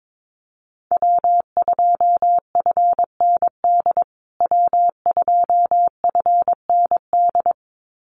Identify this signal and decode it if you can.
Morse code, heard as a keyed tone, about 22 wpm, 700 Hz. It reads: W2FND W2FND
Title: USA Call Signs Repeated 2x 22wpm